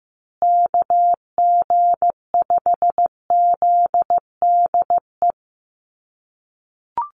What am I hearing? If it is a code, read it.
Morse code, heard as a keyed tone, about 15 wpm, 700 Hz. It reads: KG5ZDE
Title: USA Call Signs Repeated 3x 15wpm